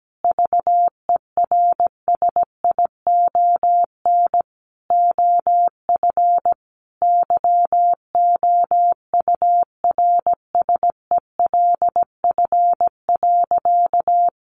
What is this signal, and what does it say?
Morse code, heard as a keyed tone, about 17 wpm, 700 Hz. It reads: VERSION OF YOURSELF.
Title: Sentences from Top 1400 Words 17wpm